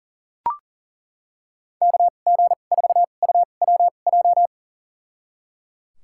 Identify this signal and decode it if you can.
Morse code, heard as a keyed tone, about 39 wpm, spaced as for 28 wpm, 700 Hz. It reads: KG4UWJ
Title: USA Call Signs - 6 Characters - 2x Character Spacing - 40wpm